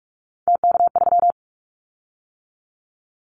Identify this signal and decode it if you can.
Morse code, heard as a keyed tone, about 44 wpm, 700 Hz. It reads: TK3
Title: Callsign Prefixes for Top 100 DXCC - 45wpm